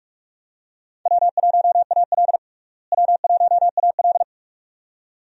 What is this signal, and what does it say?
Morse code, heard as a keyed tone, about 42 wpm, 700 Hz. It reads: W1AL W1AL
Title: USA Call Signs - 4 Characters - Repeated 2x - 45wpm